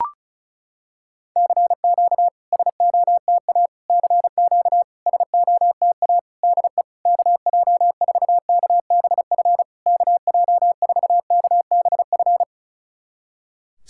Morse code, heard as a keyed tone, about 34 wpm, 700 Hz. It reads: CQ SOTA CQ SOTA DE KJ4KBF KJ4KBF
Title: SOTA QSO Practice 35wpm